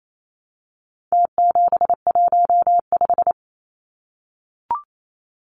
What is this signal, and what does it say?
Morse code, heard as a keyed tone, about 28 wpm, 700 Hz. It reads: T715
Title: Numbers - Contest Style - 4-Digit 28wpm